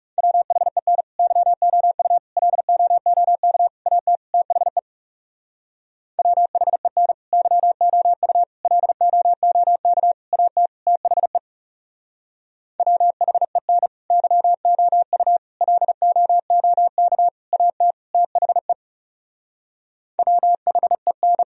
Morse code, heard as a keyed tone, about 39 wpm, 700 Hz. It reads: WHEN YOU LOOK AT THE WHEN YOU LOOK AT THE WHEN YOU LOOK AT THE WHEN